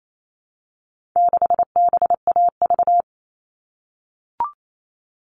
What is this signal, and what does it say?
Morse code, heard as a keyed tone, about 28 wpm, 700 Hz. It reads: 6BAV
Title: All Cut Numbers - 4-Digit 28wpm